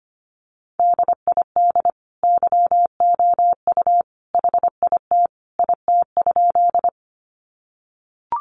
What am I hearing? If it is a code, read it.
Morse code, heard as a keyed tone, about 25 wpm, 700 Hz. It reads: DID YOU HIT IT?
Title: Sentences from Top 1200 Words 25wpm